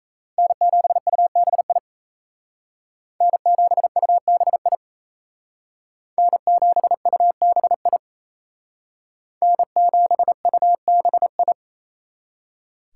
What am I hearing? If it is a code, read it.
Morse code, heard as a keyed tone, about 35 wpm, 700 Hz. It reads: N7UBI N7UBI N7UBI N7UBI
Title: Speed Racing - USA 1x3 Call Signs 28wpm